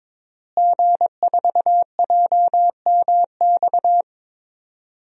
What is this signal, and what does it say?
Morse code, heard as a keyed tone, about 22 wpm, 700 Hz. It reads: G4JMX
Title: DXCC-100 Call Signs - 5 Characters - Repeated 3x - 22wpm